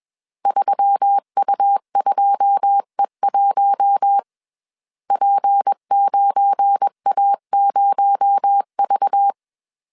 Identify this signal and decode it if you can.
Morse code, heard as a keyed tone, about 21 wpm, 800 Hz. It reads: 3U2E1 P9A0V
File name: CW / DET / 1 9 1500C S C 20wpm 2310